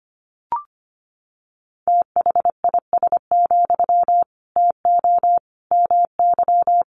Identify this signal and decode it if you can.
Morse code, heard as a keyed tone, about 25 wpm, 700 Hz. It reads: THIS, TO MY
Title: Sentences from Top 1500 Words 25wpm